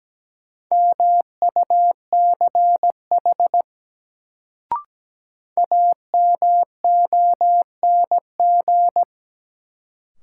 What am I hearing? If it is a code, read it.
Morse code, heard as a keyed tone, about 17 wpm, 700 Hz. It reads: MUCH AMONG
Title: Sets of 2 Words - New Word Drill - 17wpm